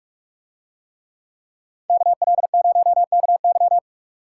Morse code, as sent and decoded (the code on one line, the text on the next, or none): -.- .-.. ----- -.- -.--
KL0KY